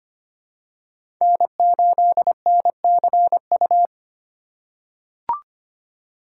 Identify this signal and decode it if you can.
Morse code, heard as a keyed tone, about 25 wpm, 700 Hz. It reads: N8NCU
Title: USA Call Signs - 5 Characters - 3x Character Spacing - 25wpm